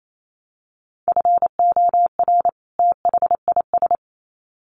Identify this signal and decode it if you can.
Morse code, heard as a keyed tone, about 28 wpm, 700 Hz. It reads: FOR THIS
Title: Sets of 2 Words 28wpm